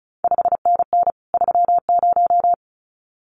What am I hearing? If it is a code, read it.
Morse code, heard as a keyed tone, about 36 wpm, 700 Hz. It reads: 5NN 30
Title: CQ WW DX Contest Exchange Encore - 35wpm